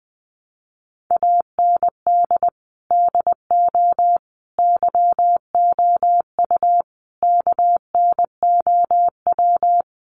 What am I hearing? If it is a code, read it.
Morse code, heard as a keyed tone, about 20 wpm, 700 Hz. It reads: AND DO YOU KNOW
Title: Sentences from Top 200 Words 20wpm